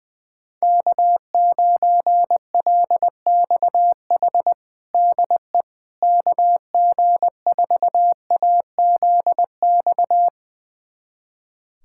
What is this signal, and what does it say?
Morse code, heard as a keyed tone, about 20 wpm, 700 Hz. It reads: K9LXH DE KG4AZX
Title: USA Callsign DE Callsign 20wpm